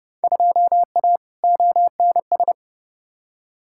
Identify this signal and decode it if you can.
Morse code, heard as a keyed tone, about 32 wpm, 700 Hz. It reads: 2A ONS
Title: Field Day Exchanges Encore - 30wpm